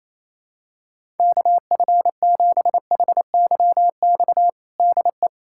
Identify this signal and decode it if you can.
Morse code, heard as a keyed tone, about 28 wpm, 700 Hz. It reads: KF7HYX DE
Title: USA Callsign DE Callsign - 6 Characters - 28wpm